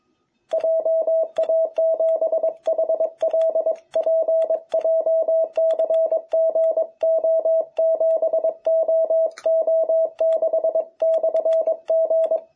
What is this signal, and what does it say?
Morse code, heard as a keyed tone, about 23 wpm, 660 Hz. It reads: JA7HLP JCGO7OO6/G
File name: jh8cbh / 2021cw